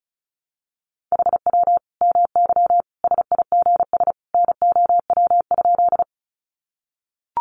Morse code, heard as a keyed tone, about 34 wpm, 700 Hz. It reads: HW MY SIGS NOW?
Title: Common QSO Phrases 35wpm